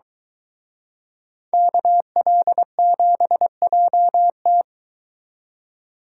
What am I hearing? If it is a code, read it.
Morse code, heard as a keyed tone, about 23 wpm, 700 Hz. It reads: KL7JT